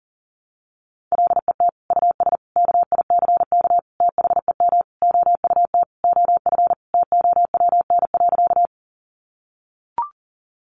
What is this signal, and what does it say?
Morse code, heard as a keyed tone, about 40 wpm, 700 Hz. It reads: LET US KICK THEM OUT OF TOWN.
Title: Sentences from Top 1300 Words 40wpm